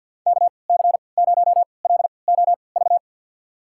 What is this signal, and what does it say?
Morse code, heard as a keyed tone, about 52 wpm, spaced as for 29 wpm, 700 Hz. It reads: KC0LOV
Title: USA Call Signs - 3x Character Spacing - 50wpm